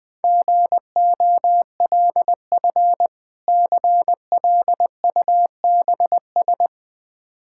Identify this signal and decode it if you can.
Morse code, heard as a keyed tone, about 20 wpm, 700 Hz. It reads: GOLF CLUBS
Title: Difficult Words with Hints - 20wpm